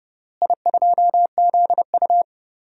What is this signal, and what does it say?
Morse code, heard as a keyed tone, about 30 wpm, 700 Hz. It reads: I2ZU
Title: DXCC-100 Call Signs - 4 Characters - Repeated 3x - 30wpm